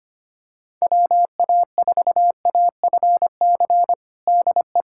Morse code, heard as a keyed tone, about 25 wpm, 700 Hz. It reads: WA4AFC DE
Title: USA Callsign DE Callsign - 6 Characters - 25wpm